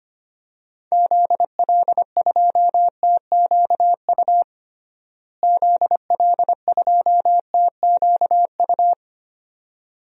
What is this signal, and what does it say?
Morse code, heard as a keyed tone, about 25 wpm, 700 Hz. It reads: ZL2TQU ZL2TQU
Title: DXCC-100 Call Signs - 6 Characters - Repeated 2x - 25wpm